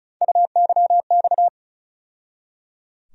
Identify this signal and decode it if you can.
Morse code, heard as a keyed tone, about 36 wpm, 700 Hz. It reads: UYX
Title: USA Call Sign Postfixes - 3 Letters 35wpm